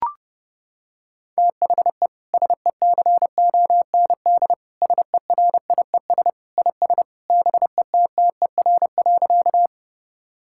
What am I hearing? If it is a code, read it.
Morse code, heard as a keyed tone, about 29 wpm, 700 Hz. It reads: THE SECOND SERIES IS BETTER.
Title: Sentences from Top 1000 Words - Review Part 5 - 30wpm